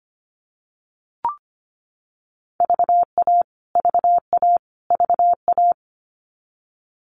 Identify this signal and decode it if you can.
Morse code, heard as a keyed tone, about 25 wpm, 700 Hz. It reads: VA VA VA
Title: US States Repeated 3x 25wpm